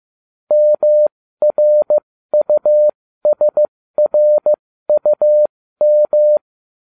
Morse code, heard as a keyed tone, about 15 wpm, spaced as for 13 wpm, 600 Hz. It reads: MRUSRUM